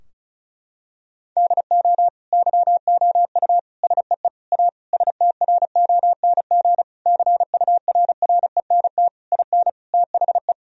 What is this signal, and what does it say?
Morse code, heard as a keyed tone, about 36 wpm, 700 Hz. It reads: DO YOU SEE A STRONG CURRENT IN THE
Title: Sentences from Top 1000 Words - Review Part 4 - 35wpm